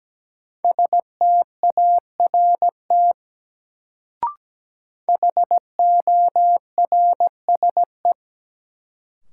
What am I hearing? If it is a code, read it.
Morse code, heard as a keyed tone, about 17 wpm, 700 Hz. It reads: START HORSE
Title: Sentences from Top 300 Words - New Word Drill - 17wpm